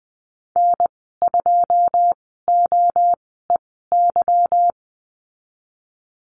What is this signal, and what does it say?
Morse code, heard as a keyed tone, about 20 wpm, spaced as for 14 wpm, 700 Hz. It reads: N2OEY
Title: 147 Lesson 24g - Letter y - Call Signs 20wpm